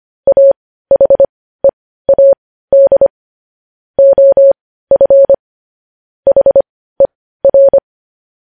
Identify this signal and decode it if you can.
Morse code, heard as a keyed tone, about 25 wpm, spaced as for 15 wpm, 550 Hz. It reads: AHEAD OF HER
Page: news headlines in morse code at 15 wpm